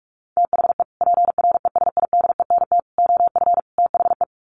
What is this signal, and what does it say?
Morse code, heard as a keyed tone, about 44 wpm, 700 Hz. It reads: THE PRESIDENT OF THE
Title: Sets of 4 Words 45wpm